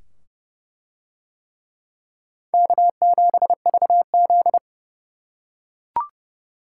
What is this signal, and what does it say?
Morse code, heard as a keyed tone, about 28 wpm, 700 Hz. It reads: K7VZ